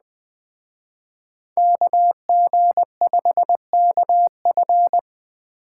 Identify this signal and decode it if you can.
Morse code, heard as a keyed tone, about 21 wpm, 700 Hz. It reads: KG5KF
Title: Speed Racing - USA 2x2 Call Signs 17wpm